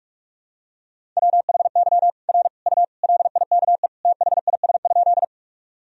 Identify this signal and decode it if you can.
Morse code, heard as a keyed tone, about 42 wpm, 700 Hz. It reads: WHY R U LIKE THIS?
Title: Sentences from Top 200 Words and Abbreviations 45wpm